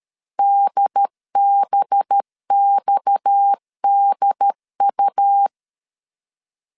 Morse code, heard as a keyed tone, about 13 wpm, 800 Hz. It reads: DBXDU
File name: CW / DET / 1 9 1500C S C 12wpm 3811